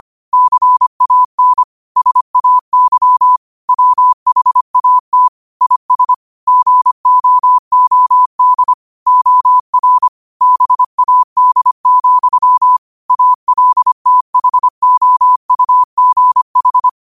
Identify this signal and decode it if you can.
Morse code, heard as a keyed tone, about 25 wpm, 1000 Hz. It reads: CAN SAY WHAT IS GOOD OR BAD, ALTHOUGH
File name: qotdfiles / 25wpm